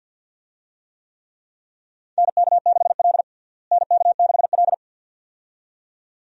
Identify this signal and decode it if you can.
Morse code, heard as a keyed tone, about 52 wpm, 700 Hz. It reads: NK6L NK6L
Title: USA Call Signs - 4 Characters - Repeated 2x - 50wpm